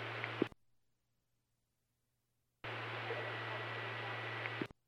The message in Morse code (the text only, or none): none